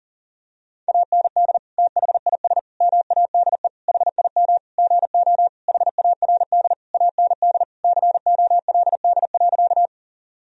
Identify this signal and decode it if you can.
Morse code, heard as a keyed tone, about 40 wpm, 700 Hz. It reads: AND THIS MADE HIM GO HARD AND COLD.
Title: Sentences from Top 1000 Words - Review Part 1 - 40wpm